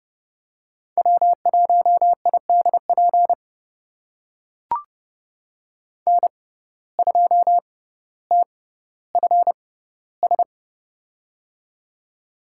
Morse code, heard as a keyed tone, about 30 wpm, spaced as for 15 wpm, 700 Hz. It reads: W1IDP N2TFS
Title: USA Call Signs - 5 Characters - 6x Character Spacing - 30wpm